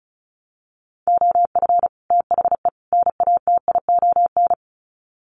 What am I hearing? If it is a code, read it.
Morse code, heard as a keyed tone, about 34 wpm, 700 Hz. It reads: OF THE NATION
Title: Sets of 3 Words 35wpm